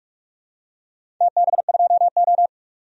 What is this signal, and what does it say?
Morse code, heard as a keyed tone, about 46 wpm, 700 Hz. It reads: TB2O
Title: DXCC-100 Call Signs - 4 Characters - 45wpm